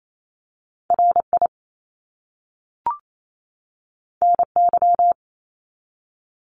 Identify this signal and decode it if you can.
Morse code, heard as a keyed tone, about 28 wpm, 700 Hz. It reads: RI NY
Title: US States 28wpm